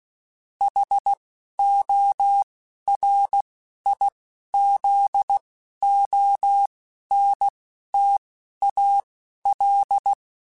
Morse code, heard as a keyed tone, about 16 wpm, spaced as for 12 wpm, 800 Hz. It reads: HORIZONTAL
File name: fromSMs / FromSMs mp3s / Antennas@10